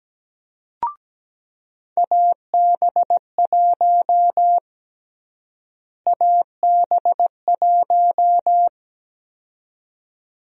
Morse code, heard as a keyed tone, about 17 wpm, 700 Hz. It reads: AB1 AB1
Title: USA Call Sign Prefixes - Repeated 2x - 17wpm